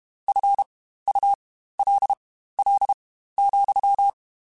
Morse code, heard as a keyed tone, about 32 wpm, spaced as for 15 wpm, 800 Hz. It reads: FULL,
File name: Spaced / Chuckles Spaced-mp3s / HalfFull sp @30